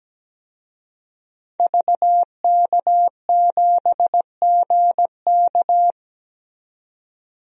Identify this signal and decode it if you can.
Morse code, heard as a keyed tone, about 17 wpm, 700 Hz. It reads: VK7GK
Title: DXCC-100 Call Signs - 5 Characters - 17wpm